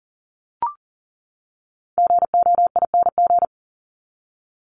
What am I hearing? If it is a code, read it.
Morse code, heard as a keyed tone, about 40 wpm, 700 Hz. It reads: GOING
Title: ING Suffix - Top 100 Words with Base Words 40wpm